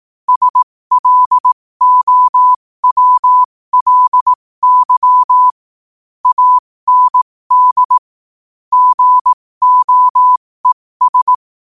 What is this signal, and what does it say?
Morse code, heard as a keyed tone, about 18 wpm, spaced as for 15 wpm, 1000 Hz. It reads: SLOWLY AND GOES
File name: qotdfiles / 15wpm